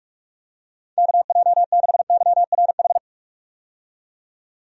Morse code, heard as a keyed tone, about 46 wpm, 700 Hz. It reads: KJ6YRH
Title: USA Call Signs - 6 Characters - 45wpm